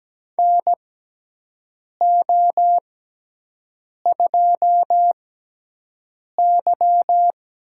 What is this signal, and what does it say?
Morse code, heard as a keyed tone, about 17 wpm, spaced as for 6 wpm, 700 Hz. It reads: NO2Y
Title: USA Call Signs - 6x Character Spacing - 17wpm